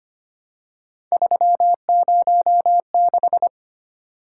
Morse code, heard as a keyed tone, about 25 wpm, 700 Hz. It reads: 306
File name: Three Letter-Number Combinations - 25wpm